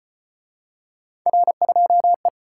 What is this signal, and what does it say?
Morse code, heard as a keyed tone, about 33 wpm, 700 Hz. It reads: R2E